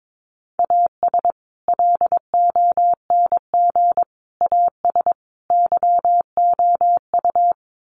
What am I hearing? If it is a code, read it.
Morse code, heard as a keyed tone, about 22 wpm, 700 Hz. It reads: AS LONG AS YOU
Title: Sets of 4 Words 22wpm